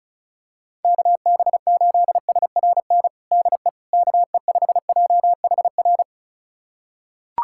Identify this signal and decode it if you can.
Morse code, heard as a keyed tone, about 34 wpm, 700 Hz. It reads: KB8SRN DE KE5JHR
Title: USA Callsign DE Callsign - 6 Characters - 35wpm